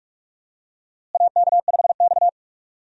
Morse code, heard as a keyed tone, about 44 wpm, 700 Hz. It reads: AK5X
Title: USA Call Signs - 4 Characters - Repeated 2x - 45wpm